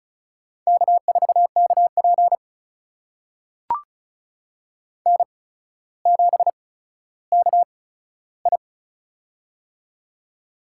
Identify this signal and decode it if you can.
Morse code, heard as a keyed tone, about 36 wpm, spaced as for 15 wpm, 700 Hz. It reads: K4KP N7KI
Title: USA Call Signs - 4 Characters - 8x Character Spacing - 35wpm